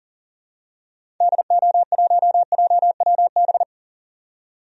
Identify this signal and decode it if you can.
Morse code, heard as a keyed tone, about 40 wpm, 700 Hz. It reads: DO1JWB
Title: DXCC-100 Call Signs - 6 Characters - 40wpm